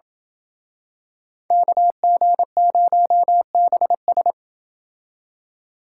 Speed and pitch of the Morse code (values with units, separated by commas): 27 wpm, 700 Hz